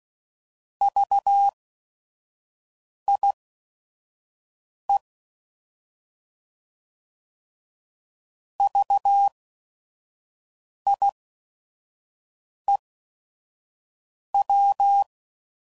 Morse code, heard as a keyed tone, about 16 wpm, spaced as for 5 wpm, 800 Hz. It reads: VIE VIEW
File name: Prefixes / By First Letter / Vs